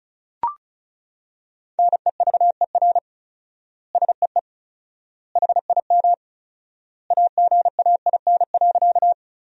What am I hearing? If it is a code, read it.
Morse code, heard as a keyed tone, about 36 wpm, 700 Hz. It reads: NEVER SEE HIM AGAIN.
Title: 4x Word-Spacing - Sentences from Top 300 Words - 35wpm